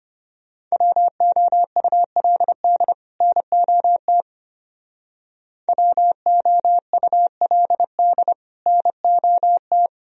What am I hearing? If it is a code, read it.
Morse code, heard as a keyed tone, about 32 wpm, 700 Hz. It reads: WOULD NOT WOULD NOT